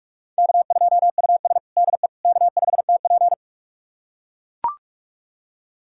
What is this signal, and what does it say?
Morse code, heard as a keyed tone, about 44 wpm, 690 Hz. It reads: K2US DE K5TP
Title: USA Callsign DE Callsign - 4 Characters - 45wpm